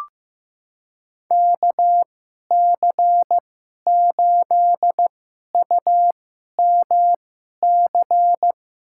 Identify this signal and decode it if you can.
Morse code, heard as a keyed tone, about 15 wpm, spaced as for 11 wpm, 700 Hz. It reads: KC8UMC
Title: USA Call Signs - 6 Characters - 2x Character Spacing - 15wpm